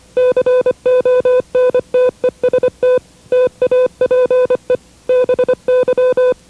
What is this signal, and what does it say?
Morse code, heard as a keyed tone, about 25 wpm, 500 Hz. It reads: CONTEST TAPE BY
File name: WRTC1990-CW-Pileup